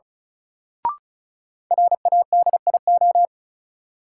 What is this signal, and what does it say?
Morse code, heard as a keyed tone, about 36 wpm, 700 Hz. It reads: RADIO
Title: Sentences from Top 1000 Words - Review Part 3 - New Word Drill - 35wpm